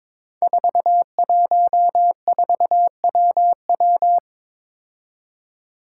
Morse code, heard as a keyed tone, about 22 wpm, 700 Hz. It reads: 414WW
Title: Five Challenging Letter-Number Combinations 22wpm